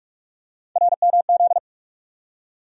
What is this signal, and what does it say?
Morse code, heard as a keyed tone, about 42 wpm, 700 Hz. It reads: RMZ
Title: USA Call Sign Postfixes - 3 Letters 45wpm